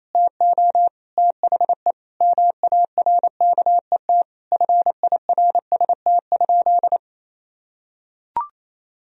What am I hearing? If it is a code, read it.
Morse code, heard as a keyed tone, about 28 wpm, 700 Hz. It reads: TO THE MARKET FIRST?
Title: Sentences from Top 1000 Words - Review Part 3 - 28wpm